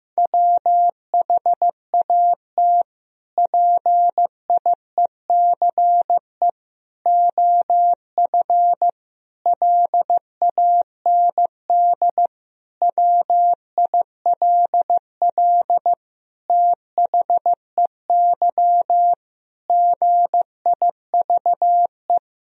Morse code, heard as a keyed tone, about 16 wpm, 700 Hz. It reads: WHAT PIECE OF LAND WILL THEY GIVE
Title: Sentences from Top 1000 Words - Review Part 2 - 15wpm